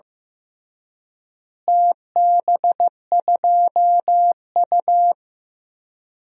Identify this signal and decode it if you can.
Morse code, heard as a keyed tone, about 15 wpm, 700 Hz. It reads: TB2U
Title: DXCC-100 Call Signs - 4 Characters - Repeated 3x - 15wpm